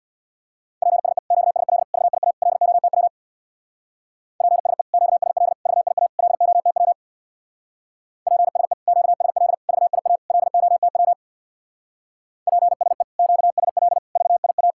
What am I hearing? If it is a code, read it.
Morse code, heard as a keyed tone, about 66 wpm, 700 Hz. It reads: PSE QSL VIA LOTW PSE QSL VIA LOTW PSE QSL VIA LOTW PSE QSL VIA